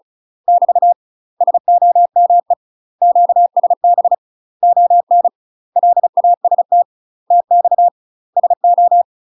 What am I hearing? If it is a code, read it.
Morse code, heard as a keyed tone, about 36 wpm, 700 Hz. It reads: <BT> SOME QSB ON LAST TX SO
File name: UK MASTER QSOs TRAINING / 35WPM / UK MASTER QSOs 0011